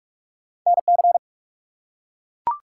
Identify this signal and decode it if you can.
Morse code, heard as a keyed tone, about 44 wpm, 700 Hz. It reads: NC
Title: US States and Canadian Provinces 45wpm